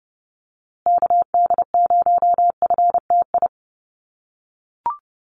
Morse code, heard as a keyed tone, about 30 wpm, 700 Hz. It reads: KD0FTI
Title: USA Call Signs Repeated 3x 30wpm